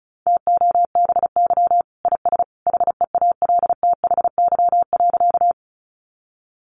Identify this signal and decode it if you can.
Morse code, heard as a keyed tone, about 34 wpm, 700 Hz. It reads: TOBY IS HEALTHY.
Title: Easy Phrases 35wpm